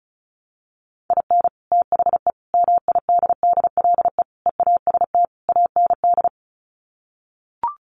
Morse code, heard as a keyed tone, about 34 wpm, 700 Hz. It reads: IN THE MIDDLE EAST AND
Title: Sets of 5 Words Encore 35wpm